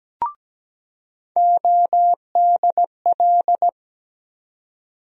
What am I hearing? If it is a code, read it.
Morse code, heard as a keyed tone, about 17 wpm, 700 Hz. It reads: ODL
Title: USA Call Sign Postfixes - 3 Letters 17wpm